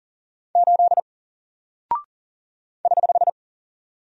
Morse code, heard as a keyed tone, about 38 wpm, 700 Hz. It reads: 8 <HH>
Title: Single Character (Letter:Number:Punctuation:Prosign) 40wpm